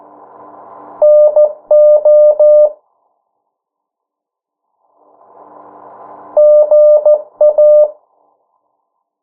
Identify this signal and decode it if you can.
Morse code, heard as a keyed tone, about 14 wpm, 600 Hz. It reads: NO GA
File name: morse mp3 / 09 urok 8